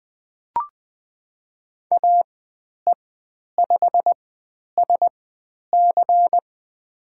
Morse code, heard as a keyed tone, about 20 wpm, spaced as for 10 wpm, 700 Hz. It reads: AE5SC
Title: 095 Lesson 16g - Letter c - Call Signs 20wpm